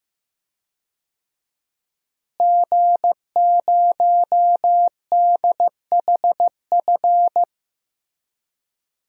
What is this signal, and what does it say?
Morse code, heard as a keyed tone, about 15 wpm, 700 Hz. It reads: G0DHF
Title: DXCC-100 Call Signs - 5 Characters - Repeated 2x - 15wpm